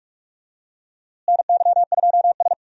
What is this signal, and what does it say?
Morse code, heard as a keyed tone, about 47 wpm, 700 Hz. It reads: NY2S
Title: USA Call Signs - 4 Characters - Repeated 2x - 45wpm